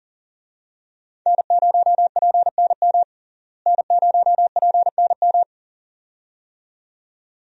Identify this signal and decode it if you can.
Morse code, heard as a keyed tone, about 40 wpm, 700 Hz. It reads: N0PNM N0PNM
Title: USA Call Signs - 5 Characters - Repeated 2x - 40wpm